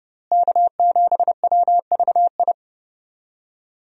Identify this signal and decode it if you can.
Morse code, heard as a keyed tone, about 28 wpm, 700 Hz. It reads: K7WVI
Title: USA Call Signs - 5 Characters - Repeated 2x - 30wpm